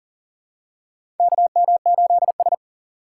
Morse code, heard as a keyed tone, about 40 wpm, 700 Hz. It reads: KM8S
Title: USA Call Signs - 4 Characters - 3x Character Spacing - 40wpm